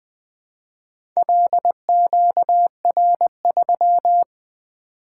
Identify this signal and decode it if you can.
Morse code, heard as a keyed tone, about 20 wpm, 700 Hz. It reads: LQR3
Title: Four Letter-Number Combinations - 20wpm